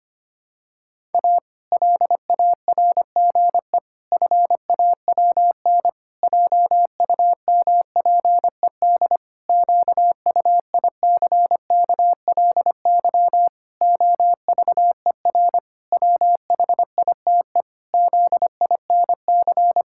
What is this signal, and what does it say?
Morse code, heard as a keyed tone, about 25 wpm, 700 Hz. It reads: A LARGE FAWN JUMPED QUICKLY OVER WHITE ZINC
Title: Pangram Challenge v2 - 25wpm